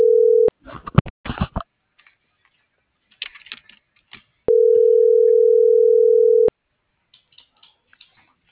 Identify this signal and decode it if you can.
Ringback tone (North America), 440 + 480 Hz.